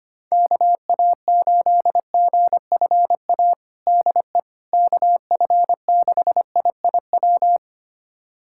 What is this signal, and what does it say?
Morse code, heard as a keyed tone, about 25 wpm, 700 Hz. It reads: KA8GFA DE KF6IIW
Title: USA Callsign DE Callsign 25wpm